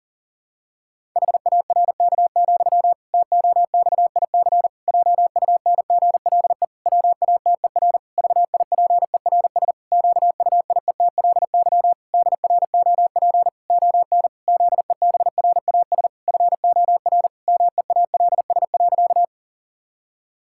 Morse code, heard as a keyed tone, about 40 wpm, 700 Hz. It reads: HARK, TOXIC JUNGLE WATER VIPERS QUIETLY DROP ON ZEBRAS FOR MEALS.
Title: Pangram Challenge v2 - 40wpm